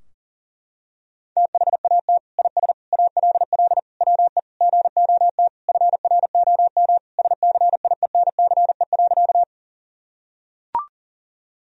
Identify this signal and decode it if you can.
Morse code, heard as a keyed tone, about 37 wpm, 700 Hz. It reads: THAT IS ALL WE GOT FROM SCIENCE.
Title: Sentences from Top 1000 Words - Review Part 2 - 40wpm